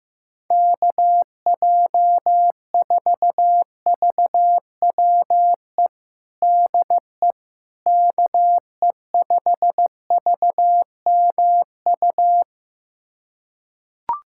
Morse code, heard as a keyed tone, about 15 wpm, 700 Hz. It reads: KJ4VWE DE KE5VMU
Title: USA Callsign DE Callsign - 6 Characters - 15wpm